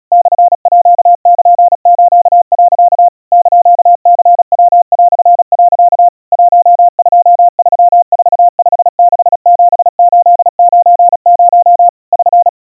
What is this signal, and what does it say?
Morse code (Japanese Wabun code), heard as a keyed tone, about 37 wpm, 700 Hz. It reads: モールス、 <KN>ニヤ)、 1234567890 <SN>
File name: iroha 120 cpm 700 Hz Crispy